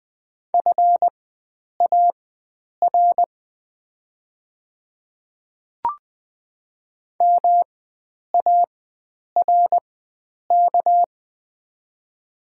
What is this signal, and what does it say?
Morse code, heard as a keyed tone, about 20 wpm, spaced as for 9 wpm, 700 Hz. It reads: FAR MARK
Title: 4x Character Spacing - Top 300 Words - 20wpm